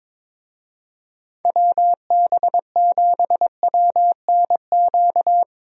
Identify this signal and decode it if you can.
Morse code, heard as a keyed tone, about 22 wpm, 700 Hz. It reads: WB7WNQ
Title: USA Call Signs - 6 Characters - 22wpm